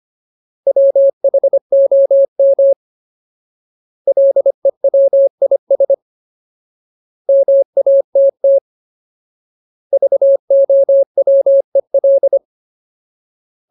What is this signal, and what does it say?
Morse code, heard as a keyed tone, about 25 wpm, 550 Hz. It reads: WHOM LEWIS MATT VOWEL
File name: Ultimate Word List / 25WPM Ultimate